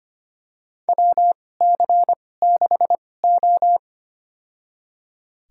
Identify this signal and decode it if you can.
Morse code, heard as a keyed tone, about 25 wpm, spaced as for 18 wpm, 700 Hz. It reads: WC6O
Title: USA Call Signs - 4 Characters - 2x Character Spacing - 25wpm